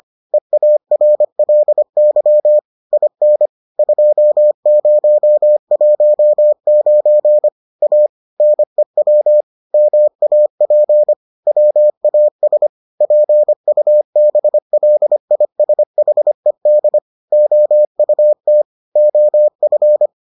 Morse code, heard as a keyed tone, about 25 wpm, 600 Hz. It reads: EARLY IN 2019 A NEW MAP WAS PUBLISHED OUT OF